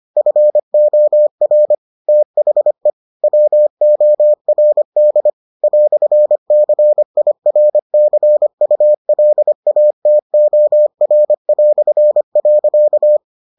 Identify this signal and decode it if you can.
Morse code, heard as a keyed tone, about 25 wpm, 600 Hz. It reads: FOR THE WORD "CIRCULATOR".